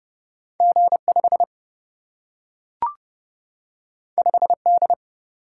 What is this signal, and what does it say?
Morse code, heard as a keyed tone, about 30 wpm, 700 Hz. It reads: G5 5D
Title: Single-Letter Single-Number 30wpm